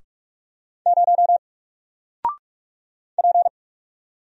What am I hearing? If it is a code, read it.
Morse code, heard as a keyed tone, about 44 wpm, 700 Hz. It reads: 0 P
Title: Single Letter-Number Alt 45wpm